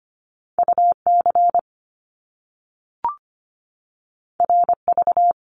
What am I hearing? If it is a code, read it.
Morse code, heard as a keyed tone, about 25 wpm, 700 Hz. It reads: UC RV
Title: Top Two Letter Combinations in Words 25wpm